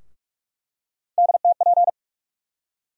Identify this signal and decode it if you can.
Morse code, heard as a keyed tone, about 47 wpm, 700 Hz. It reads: DTP
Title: USA Call Sign Postfixes - 3 Letters 45wpm